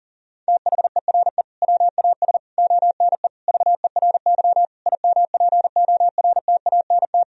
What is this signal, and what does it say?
Morse code, heard as a keyed tone, about 40 wpm, 700 Hz. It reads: THERE WAS ONE VERY IMPORTANT
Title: Sentences from Top 1000 Words - Review Part 3 - 40wpm